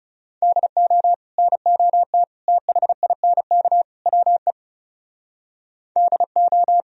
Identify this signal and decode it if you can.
Morse code, heard as a keyed tone, about 38 wpm, 700 Hz. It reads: DO NOT THINK WE DO